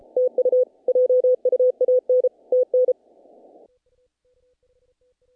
Morse code, heard as a keyed tone, about 33 wpm, 560 Hz. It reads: TU JUAN TN
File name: contests / naqp-cw-k5tr-op-k5pi- -1800